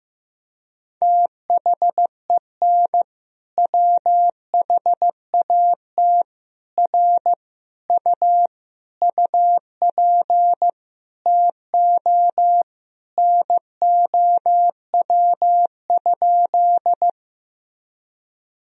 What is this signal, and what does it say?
Morse code, heard as a keyed tone, about 15 wpm, 700 Hz. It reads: THEN WHAT R U UP TO NOW?
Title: Sentences from Top 100 Words and Abbreviations 15wpm